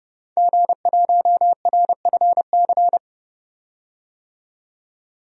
Morse code, heard as a keyed tone, about 30 wpm, 700 Hz. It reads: G1RFC
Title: DXCC-100 Call Signs - 5 Characters - 30wpm